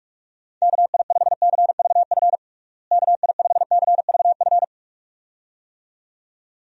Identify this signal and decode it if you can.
Morse code, heard as a keyed tone, about 42 wpm, 700 Hz. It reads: KI5CVF KI5CVF
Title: USA Call Signs - 6 Characters - Repeated 2x - 45wpm